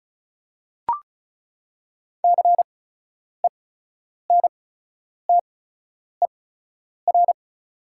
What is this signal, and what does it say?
Morse code, heard as a keyed tone, about 34 wpm, spaced as for 10 wpm, 690 Hz. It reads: CENTER
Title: 8x Character Spacing - Top 500 Words - 35wpm